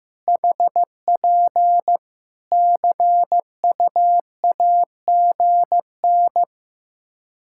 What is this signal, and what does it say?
Morse code, heard as a keyed tone, about 15 wpm, 700 Hz. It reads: HP CUAGN
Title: Practice QSO - Exchange 4 - Ending - 15wpm